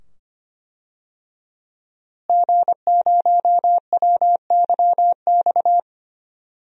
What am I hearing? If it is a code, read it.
Morse code, heard as a keyed tone, about 25 wpm, 700 Hz. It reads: G0WYX
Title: DXCC-100 Call Signs - 5 Characters - Repeated 2x - 25wpm